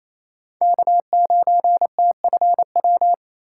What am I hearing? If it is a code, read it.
Morse code, heard as a keyed tone, about 28 wpm, 700 Hz. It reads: K9TFW
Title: USA Call Signs - 5 Characters - 2x Character Spacing - 28wpm